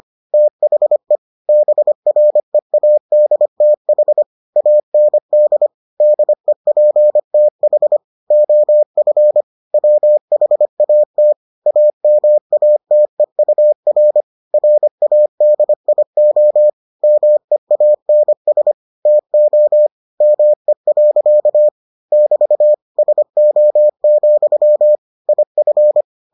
Morse code, heard as a keyed tone, about 25 wpm, 600 Hz. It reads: THE BREADTH AND DEPTH OF WHAT AMATEUR RADIO MEANS TO ME. <BT> SO, IF